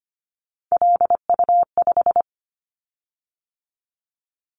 Morse code, heard as a keyed tone, about 25 wpm, 700 Hz. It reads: LU5
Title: Callsign Prefixes for Top 50 DXCC - 25wpm